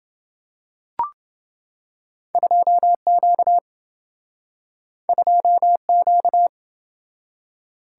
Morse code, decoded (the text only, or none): 2Q 2Q